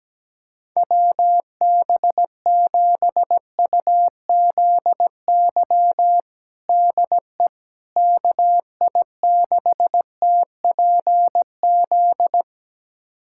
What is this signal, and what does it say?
Morse code, heard as a keyed tone, about 17 wpm, 700 Hz. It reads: WB7UZY DE KI6TPZ
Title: USA Callsign DE Callsign - 6 Characters - 17wpm